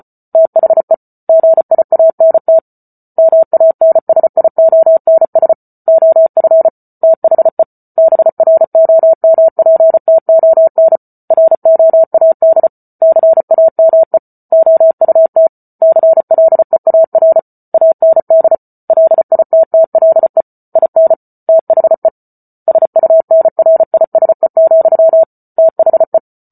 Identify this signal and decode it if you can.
Morse code, heard as a keyed tone, about 34 wpm, 650 Hz. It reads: THE GIANT MANSIONS OF THE BROMPTON ROAD CAME OUT CLEAR AND LITTLE IN THE SUNRISE, THE